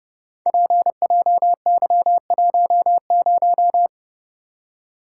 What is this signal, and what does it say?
Morse code, heard as a keyed tone, about 30 wpm, 700 Hz. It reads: PJY10
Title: Code Groups - 5 Letter:Numbers - 30wpm